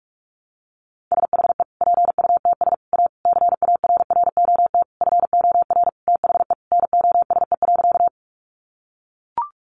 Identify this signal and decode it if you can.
Morse code, heard as a keyed tone, about 44 wpm, 700 Hz. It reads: SHE PUTS A CARROT FOR THE NOSE.
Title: Easy Phrases 45wpm